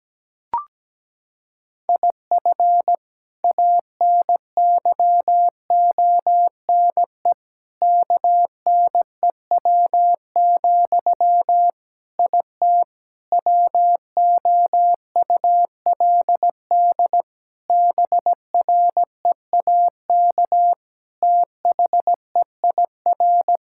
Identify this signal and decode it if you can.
Morse code, heard as a keyed tone, about 17 wpm, 700 Hz. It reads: IF ANYONE KNEW, IT WOULD BREAK THEIR
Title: Sentences from Top 1000 Words - Review Part 6 - 17wpm